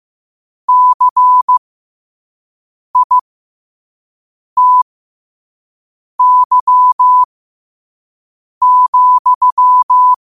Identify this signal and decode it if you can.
Morse code, heard as a keyed tone, about 15 wpm, spaced as for 5 wpm, 1000 Hz. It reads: CITY,
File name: qotdfiles / 5wpm